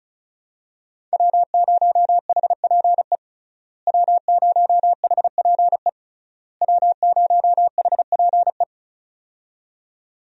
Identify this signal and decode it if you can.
Morse code, heard as a keyed tone, about 34 wpm, 700 Hz. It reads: W0HPE W0HPE W0HPE
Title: USA Call Signs - 5 Characters - Repeated 3x - 35wpm